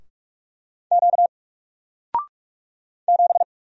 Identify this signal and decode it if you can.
Morse code, heard as a keyed tone, about 46 wpm, 700 Hz. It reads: Q 7
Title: Single Letter-Number 45wpm